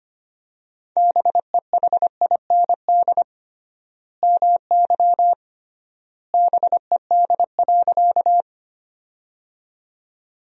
Morse code, heard as a keyed tone, about 25 wpm, 700 Hz. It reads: BEHIND MY BED.
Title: 3x Word-Spacing - Sentences from Top 500 Words - 25wpm